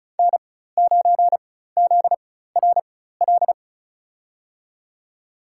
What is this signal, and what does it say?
Morse code, heard as a keyed tone, about 34 wpm, spaced as for 16 wpm, 700 Hz. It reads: N9ZRL
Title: USA Call Signs - 5 Characters - 4x Character Spacing - 35wpm